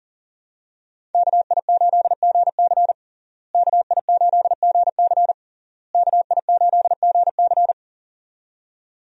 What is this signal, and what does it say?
Morse code, heard as a keyed tone, about 40 wpm, 700 Hz. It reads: KI8GC KI8GC KI8GC
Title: USA Call Signs - 5 Characters - Repeated 3x - 40wpm